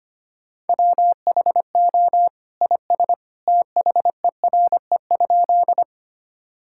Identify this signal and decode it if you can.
Morse code, heard as a keyed tone, about 25 wpm, 700 Hz. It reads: WHO IS THERE?
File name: practice / 25wpm / Sentences with 2-Word Phrases - 25wpm